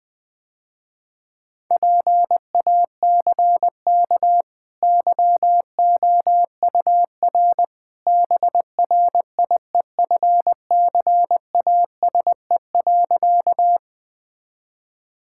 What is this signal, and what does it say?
Morse code, heard as a keyed tone, about 20 wpm, 700 Hz. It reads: PACK YOUR BRIEFCASE.
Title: Easy Sentences with Hints - 20wpm